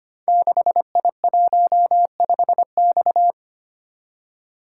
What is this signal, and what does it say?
Morse code, heard as a keyed tone, about 25 wpm, 700 Hz. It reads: 6I15X
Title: Code Groups - 5 Letter:Numbers - 25wpm